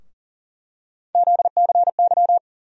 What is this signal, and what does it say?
Morse code, heard as a keyed tone, about 40 wpm, 700 Hz. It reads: ZCY
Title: USA Call Sign Postfixes - 3 Letters 40wpm